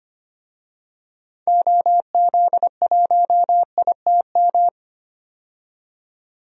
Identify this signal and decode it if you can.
Morse code, heard as a keyed tone, about 25 wpm, 700 Hz. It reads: OZ1ITM
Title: DXCC-100 Call Signs - 6 Characters - 25wpm